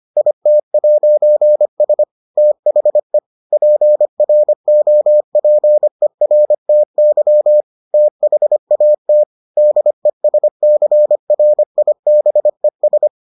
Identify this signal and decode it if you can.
Morse code, heard as a keyed tone, about 25 wpm, 600 Hz. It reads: IT'S THE PROPERTY THAT DESCRIBES